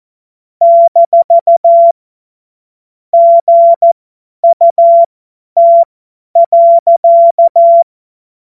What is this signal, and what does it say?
Morse code, heard as a keyed tone, about 14 wpm, spaced as for 10 wpm, 680 Hz. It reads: - GUT.
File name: Morsekurs / Witze / Witz 30 Januar 2024 50 F70